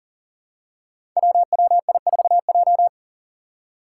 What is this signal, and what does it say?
Morse code, heard as a keyed tone, about 40 wpm, 700 Hz. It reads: WWI4J
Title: Five Challenging Letter-Number Combinations 40wpm